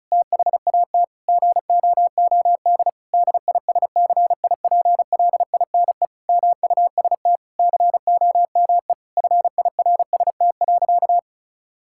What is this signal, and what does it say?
Morse code, heard as a keyed tone, about 36 wpm, 700 Hz. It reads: THAT GOOD DISCIPLINE MUST COME FIRST.
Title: Sentences from Top 1000 Words - Review Part 6 - 35wpm